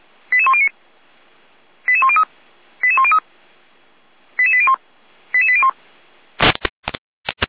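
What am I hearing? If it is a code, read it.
Five-tone selective call (ZVEI), heard as quick runs of five tones, tones of about 70 ms. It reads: 88109 88103 88103 88801 88801
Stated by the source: ZVEI five-tone call received by SDR, Feuerwehr Orlamünde.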